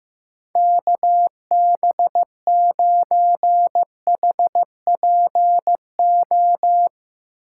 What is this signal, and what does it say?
Morse code, heard as a keyed tone, about 15 wpm, 700 Hz. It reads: KB9HPO
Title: USA Call Signs - 8x Character Spacing - 15wpm